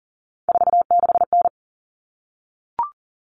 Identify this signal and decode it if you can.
Morse code, heard as a keyed tone, about 40 wpm, 700 Hz. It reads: 46N
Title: All Cut Numbers - 3-Digit 40wpm